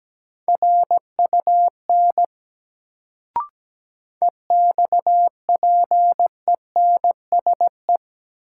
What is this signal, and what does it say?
Morse code, heard as a keyed tone, about 17 wpm, 700 Hz. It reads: RUN EXPENSE
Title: Sets of 4 Words - New Word Drill - 17wpm